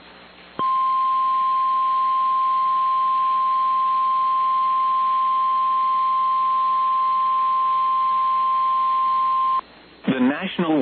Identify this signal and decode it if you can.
Emergency Alert System (SAME) tones (the 1050 Hz attention tone), with speech. It is heard from 0.5 to 9.6 s.
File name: EAS Logs / KMNR